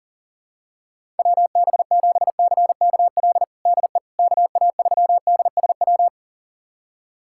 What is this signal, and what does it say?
Morse code, heard as a keyed tone, about 37 wpm, 700 Hz. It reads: WB7CKL DE KA3DSW
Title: USA Callsign DE Callsign - 6 Characters - 40wpm